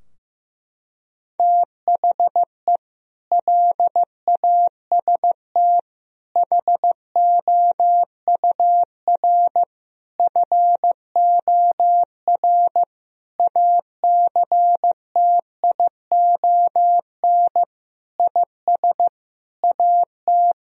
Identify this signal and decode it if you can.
Morse code, heard as a keyed tone, about 15 wpm, 700 Hz. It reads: THE LAST HOUR FOR ACTION IS AT
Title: Sentences from Top 1000 Words - Review Part 4 - 15wpm